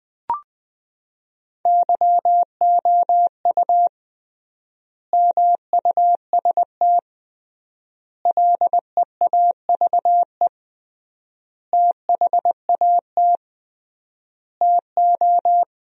Morse code, heard as a keyed tone, about 20 wpm, 700 Hz. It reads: YOU MUST LEAVE THAT TO
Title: 3x Word-Spacing - Sentences from Top 400 Words - 20wpm